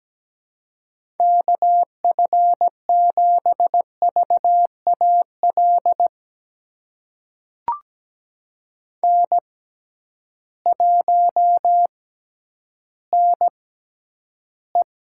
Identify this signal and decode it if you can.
Morse code, heard as a keyed tone, about 17 wpm, spaced as for 10 wpm, 700 Hz. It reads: KF7VAL N1NE
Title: USA Call Signs - 6x Character Spacing - 17wpm